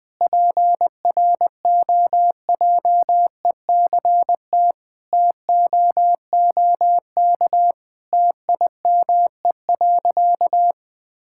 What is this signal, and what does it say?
Morse code, heard as a keyed tone, about 20 wpm, 700 Hz. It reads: PROJECT TOOK TIME.
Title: Sentences with 3-Word Phrases - 20wpm